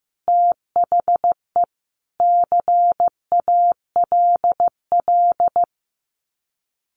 Morse code, heard as a keyed tone, about 15 wpm, 700 Hz. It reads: THE CALL
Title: Common QSO Phrases 15wpm